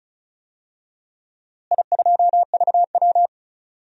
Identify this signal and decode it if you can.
Morse code, heard as a keyed tone, about 36 wpm, 700 Hz. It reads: I2VW
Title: DXCC-100 Call Signs - 4 Characters - Repeated 3x - 35wpm